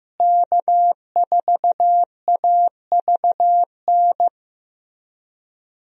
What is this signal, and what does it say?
Morse code, heard as a keyed tone, about 15 wpm, 700 Hz. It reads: K4AVN
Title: USA Call Signs - 5 Characters - Repeated 3x - 15wpm